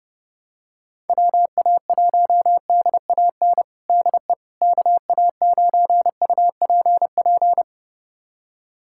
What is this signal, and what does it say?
Morse code, heard as a keyed tone, about 30 wpm, 700 Hz. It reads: WA1DAN DE KA9UPP
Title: USA Callsign DE Callsign 30wpm